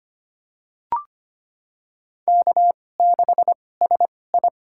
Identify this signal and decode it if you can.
Morse code, heard as a keyed tone, about 25 wpm, spaced as for 18 wpm, 700 Hz. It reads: K6SI
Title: USA Call Signs - 4 Characters - 2x Character Spacing - 25wpm